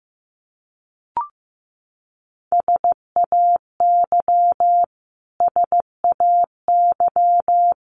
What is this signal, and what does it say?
Morse code, heard as a keyed tone, about 15 wpm, 700 Hz. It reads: SAY SAY
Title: Top 200 Words Repeated 15wpm